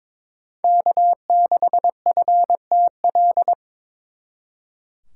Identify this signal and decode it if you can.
Morse code, heard as a keyed tone, about 26 wpm, 700 Hz. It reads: K6FTL
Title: Speed Racing - USA 1x3 Call Signs 22wpm